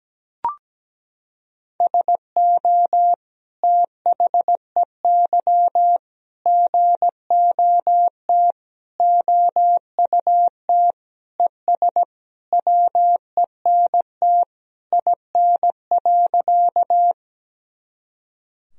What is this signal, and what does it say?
Morse code, heard as a keyed tone, about 17 wpm, 700 Hz. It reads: SO THEY GOT OUT ES WENT IN.
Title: Sentences from Top 300 Words and Abbreviations 17wpm